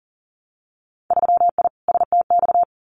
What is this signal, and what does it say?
Morse code, heard as a keyed tone, about 38 wpm, 700 Hz. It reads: 3I STX
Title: Winter Field Day Exchanges 40wpm